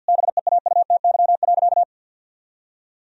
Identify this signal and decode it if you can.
Morse code, heard as a keyed tone, about 50 wpm, 700 Hz. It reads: BEAUTY.
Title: Sentences from Top 1000 Words - Review Part 2 - 50wpm